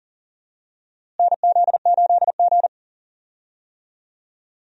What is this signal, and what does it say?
Morse code, heard as a keyed tone, about 40 wpm, 700 Hz. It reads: NZ8G
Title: USA Call Signs - 4 Characters - Repeated 3x - 40wpm